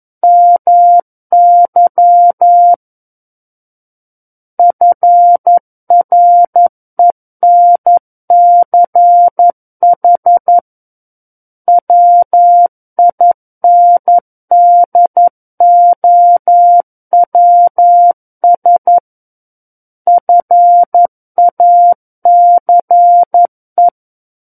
Morse code, heard as a keyed tone, about 11 wpm, 700 Hz. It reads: MY FRENCH WINDOWS FACE
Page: War of the Worlds - 02-Chapter 2 - 11 WPM